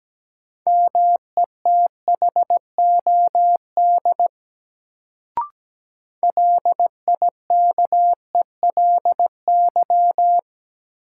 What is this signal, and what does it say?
Morse code, heard as a keyed tone, about 17 wpm, 700 Hz. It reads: METHOD LIKELY
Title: Sentences from Top 600 Words - New Word Drill - 17wpm